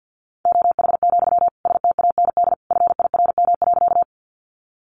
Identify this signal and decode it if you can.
Morse code, heard as a keyed tone, about 50 wpm, 700 Hz. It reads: OH, STAND FIRM.
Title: Sentences from Top 1400 Words 50wpm